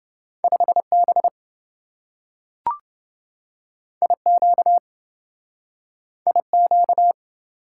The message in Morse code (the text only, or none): 5B IQ IQ